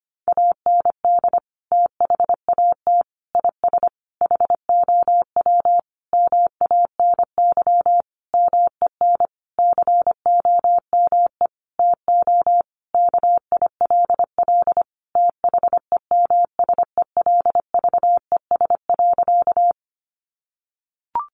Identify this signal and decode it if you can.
Morse code, heard as a keyed tone, about 25 wpm, 700 Hz. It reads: AND THAT IS HOW MANY MEN COME TO KILL THEMSELVES.
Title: Sentences from Top 1400 Words 25wpm